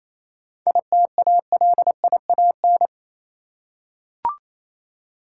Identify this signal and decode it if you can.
Morse code, heard as a keyed tone, about 28 wpm, 690 Hz. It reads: ITALIAN